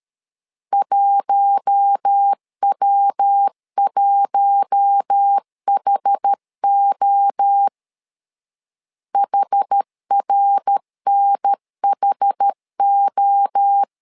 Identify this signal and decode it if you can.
Morse code, heard as a keyed tone, about 13 wpm, 800 Hz. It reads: 1W1HO HRNHO
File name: LEZIONI / 4 Q1 12wpm 2028